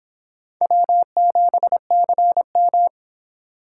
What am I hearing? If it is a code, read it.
Morse code, heard as a keyed tone, about 26 wpm, 700 Hz. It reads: W7CM